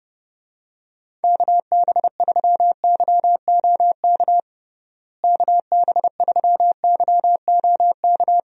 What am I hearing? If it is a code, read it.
Morse code, heard as a keyed tone, about 29 wpm, 700 Hz. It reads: KB3YOK KB3YOK
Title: USA Call Signs - 6 Characters - Repeated 3x - 30wpm